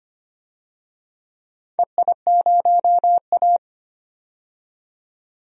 Morse code, heard as a keyed tone, about 25 wpm, 700 Hz. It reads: EI0A
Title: DXCC-100 Call Signs - 4 Characters - 25wpm